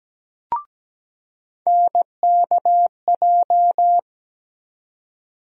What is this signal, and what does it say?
Morse code, heard as a keyed tone, about 17 wpm, 700 Hz. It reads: NKJ
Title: USA Call Sign Postfixes - 3 Letters 17wpm